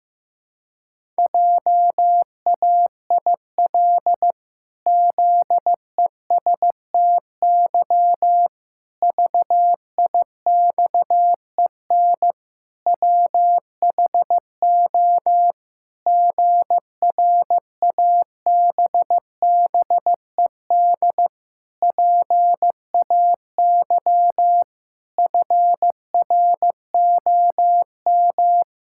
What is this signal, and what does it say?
Morse code, heard as a keyed tone, about 15 wpm, 700 Hz. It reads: JAIL ZESTY VIXEN WHO GRABBED PAY FROM
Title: Pangram Challenge v2 - 15wpm